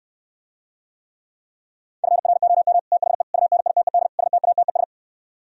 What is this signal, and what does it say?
Morse code, heard as a keyed tone, about 66 wpm, 700 Hz. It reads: FROM THE UNITED STATES